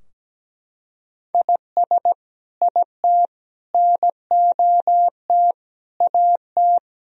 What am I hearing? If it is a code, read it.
Morse code, heard as a keyed tone, about 17 wpm, 700 Hz. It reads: IS IT NOT AT
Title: Sentences from Top 1000 Words - Review Part 4 - 17wpm